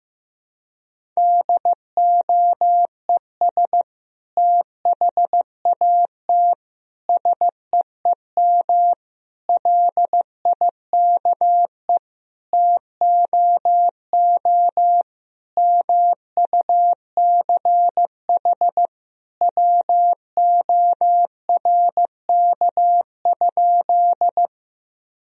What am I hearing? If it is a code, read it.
Morse code, heard as a keyed tone, about 15 wpm, 700 Hz. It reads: DOES THAT SEEM LIKE TOO MUCH WORK?
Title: Sentences from Top 1000 Words - Review Part 1 - 15wpm